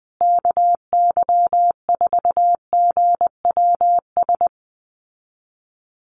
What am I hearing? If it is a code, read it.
Morse code, heard as a keyed tone, about 20 wpm, 700 Hz. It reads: KY4GWS
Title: USA Call Signs Repeated 3x 20wpm